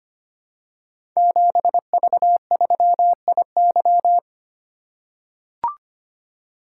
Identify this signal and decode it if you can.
Morse code, heard as a keyed tone, about 25 wpm, 690 Hz. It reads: 7V3IY
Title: Code Groups - 5 Letter:Numbers - 25wpm